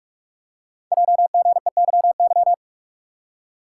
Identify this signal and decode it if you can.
Morse code, heard as a keyed tone, about 44 wpm, 690 Hz. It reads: JGEYY
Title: Code Groups - 5 Letters - 45wpm